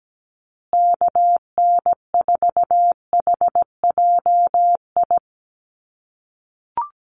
Morse code, heard as a keyed tone, about 17 wpm, 700 Hz. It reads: KN4HJI
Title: USA Call Signs Repeated 2x 17wpm